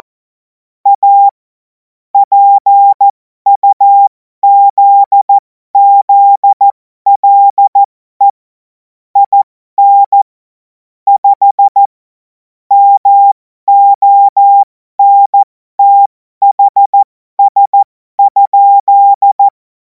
Morse code, heard as a keyed tone, about 14 wpm, spaced as for 12 wpm, 800 Hz. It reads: A PUZZLE IN 5 MONTHS?
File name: Morsekurs / Witze